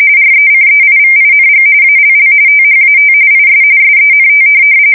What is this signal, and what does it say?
RTTY, about 45.45 baud. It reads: RECEIVING ALL THAT IS SENT
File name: fsk170 / printing-telegraph